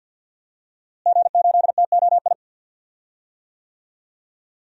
Morse code, heard as a keyed tone, about 50 wpm, 700 Hz. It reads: G8TOI
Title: DXCC-100 Call Signs - 5 Characters - Repeated 3x - 50wpm